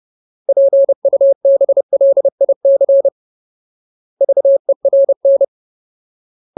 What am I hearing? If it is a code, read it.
Morse code, heard as a keyed tone, about 30 wpm, 550 Hz. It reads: PUBLIC VERN
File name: Ultimate Word List / 30WPM Ultimate